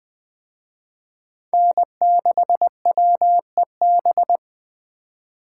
Morse code, heard as a keyed tone, about 20 wpm, 700 Hz. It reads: N6WEB
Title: Speed Racing - USA Call Signs 17wpm